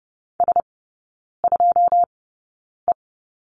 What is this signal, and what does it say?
Morse code, heard as a keyed tone, about 30 wpm, spaced as for 9 wpm, 700 Hz. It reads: S2E
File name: 081 Lesson 14e - Number 2 - Three Characters 30wpm